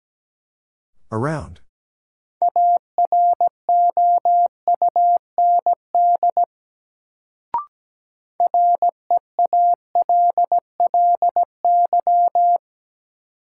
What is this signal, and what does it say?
Morse code, heard as a keyed tone, about 17 wpm, 700 Hz. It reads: AROUND REALLY
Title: Sentences from Top 600 Words - New Word Drill - 17wpm